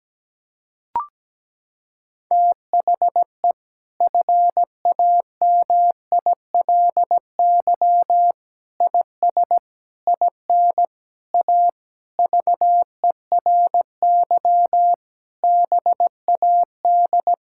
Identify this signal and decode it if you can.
Morse code, heard as a keyed tone, about 17 wpm, 700 Hz. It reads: THE FAMILY IS IN A VERY BAD
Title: Sentences from Top 1000 Words - Review Part 5 - 17wpm